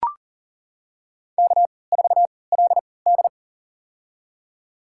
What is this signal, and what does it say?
Morse code, heard as a keyed tone, about 40 wpm, spaced as for 23 wpm, 700 Hz. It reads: K4LD
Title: USA Call Signs - 3x Character Spacing - 40wpm